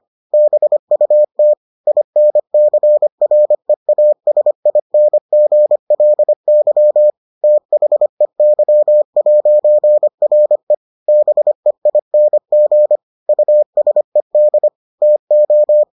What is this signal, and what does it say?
Morse code, heard as a keyed tone, about 25 wpm, 600 Hz. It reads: BUT INCREASINGLY THEY'RE BEING USED TO